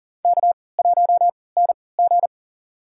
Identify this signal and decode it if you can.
Morse code, heard as a keyed tone, about 40 wpm, spaced as for 23 wpm, 700 Hz. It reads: K1NG
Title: USA Call Signs - 4 Characters - 3x Character Spacing - 40wpm